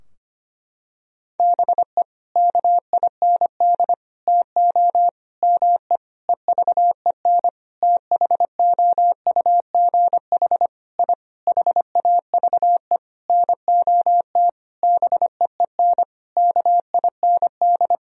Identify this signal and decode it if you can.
Morse code, heard as a keyed tone, about 25 wpm, 700 Hz. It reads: BE KIND TO ME EVEN THOUGH I HAVE NOT BEEN KIND
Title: Sentences from Top 1000 Words - Review Part 2 - 25wpm